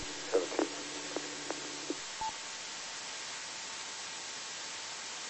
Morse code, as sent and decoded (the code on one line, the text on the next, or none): none